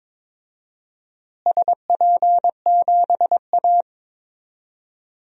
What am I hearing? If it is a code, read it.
Morse code, heard as a keyed tone, about 22 wpm, 700 Hz. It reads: SP7A
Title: DXCC-100 Call Signs - 4 Characters - Repeated 2x - 22wpm